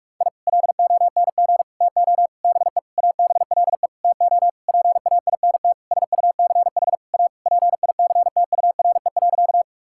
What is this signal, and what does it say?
Morse code, heard as a keyed tone, about 42 wpm, 700 Hz. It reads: I LONG TO BE ABLE TO PAINT SUCH A PICTURE.
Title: Sentences from Top 1000 Words - Review Part 6 - 45wpm